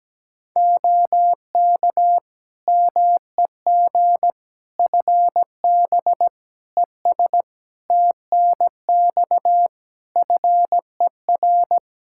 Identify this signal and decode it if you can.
Morse code, heard as a keyed tone, about 17 wpm, 700 Hz. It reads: OK MEG FB ES TNX FER
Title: Practice QSO - Exchange 2 - Info - 17wpm